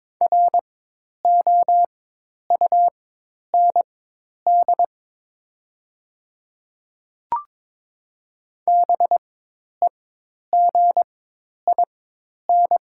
Morse code, heard as a keyed tone, about 22 wpm, spaced as for 10 wpm, 700 Hz. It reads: ROUND BEGIN
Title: 4x Character Spacing - Top 500 Words - 22wpm